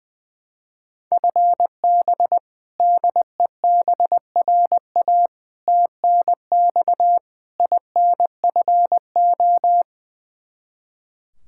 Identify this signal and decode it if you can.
Morse code, heard as a keyed tone, about 21 wpm, 700 Hz. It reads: FB DEBRA TNX INFO
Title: Practice QSO - Exchange 3 - More Info - 20wpm